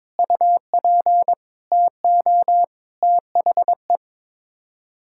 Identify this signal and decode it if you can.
Morse code, heard as a keyed tone, about 22 wpm, 700 Hz. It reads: UP TO THE
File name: practice / 22wpm / Sets of 7 Words - 22wpm